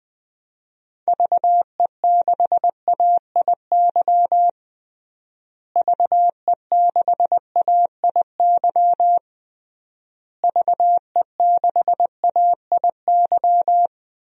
Morse code, heard as a keyed tone, about 20 wpm, 700 Hz. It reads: VE6AIY VE6AIY VE6AIY
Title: DXCC-100 Call Signs - 6 Characters - Repeated 3x - 20wpm